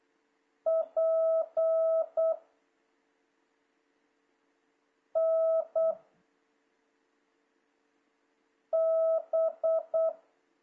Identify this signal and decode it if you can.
Morse code, heard as a keyed tone, about 8 wpm, 660 Hz. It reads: P N B
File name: jh8cbh / 2026cw